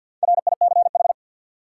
Morse code, heard as a keyed tone, about 52 wpm, 700 Hz. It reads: RICH
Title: Sentences from Top 1100 Words - New Word Drill - 50wpm